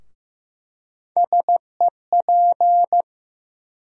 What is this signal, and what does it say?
Morse code, heard as a keyed tone, about 15 wpm, 700 Hz. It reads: SEP
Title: USA Call Sign Postfixes - 3 Letters 15wpm